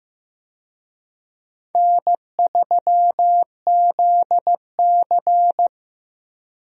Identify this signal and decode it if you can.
Morse code, heard as a keyed tone, about 15 wpm, 700 Hz. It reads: N3ZC
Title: USA Call Signs - 4 Characters - 15wpm